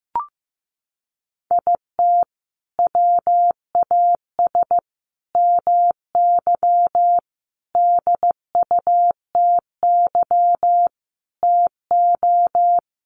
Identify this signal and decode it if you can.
Morse code, heard as a keyed tone, about 15 wpm, 700 Hz. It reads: IT WAS MY DUTY TO
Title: Sentences from Top 1500 Words 15wpm